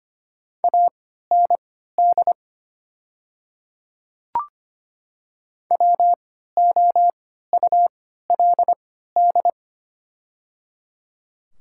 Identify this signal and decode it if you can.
Morse code, heard as a keyed tone, about 25 wpm, spaced as for 14 wpm, 700 Hz. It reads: AND WOULD
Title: 3x Character Spacing - Top 300 Words - 25wpm